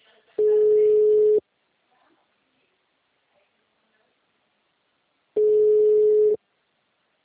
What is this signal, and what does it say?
Ringback tone (Europe, 425 Hz), with speech.